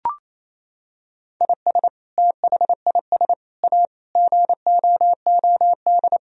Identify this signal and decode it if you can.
Morse code, heard as a keyed tone, about 28 wpm, 700 Hz. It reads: IS THIS A GOOD
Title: Sentences from Top 1000 Words - Review Part 4 - 28wpm